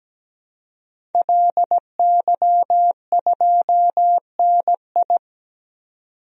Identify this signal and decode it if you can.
Morse code, heard as a keyed tone, about 17 wpm, 700 Hz. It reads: LY2NI
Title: DXCC-100 Call Signs - 5 Characters - 17wpm